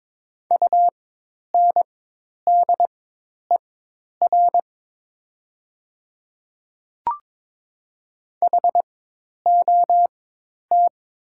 Morse code, heard as a keyed tone, about 22 wpm, spaced as for 10 wpm, 700 Hz. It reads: UNDER HOT
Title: 4x Character Spacing - Top 500 Words - 22wpm